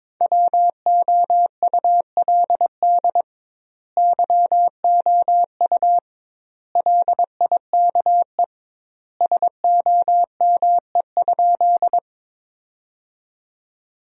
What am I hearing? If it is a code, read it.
Morse code, heard as a keyed tone, about 22 wpm, 700 Hz. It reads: WOULD YOU LIKE SOME?
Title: 2x Word-Spacing - Sentences from Top 100 Words - 22wpm